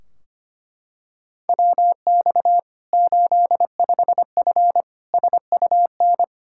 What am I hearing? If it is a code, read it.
Morse code, heard as a keyed tone, about 25 wpm, 700 Hz. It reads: WX 85F SUN
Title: Practice QSO - Exchange 2 - Info - 25wpm